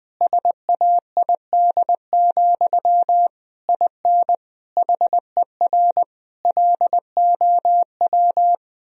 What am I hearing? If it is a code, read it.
Morse code, heard as a keyed tone, about 20 wpm, 700 Hz. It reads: SAID, IN HER LOW
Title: Sentences from Top 1000 Words - Review Part 1 - 20wpm